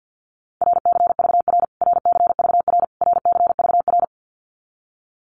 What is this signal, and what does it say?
Morse code, heard as a keyed tone, about 50 wpm, 700 Hz. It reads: RCVR RCVR RCVR
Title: Abbreviations Repeated 3x - 50wpm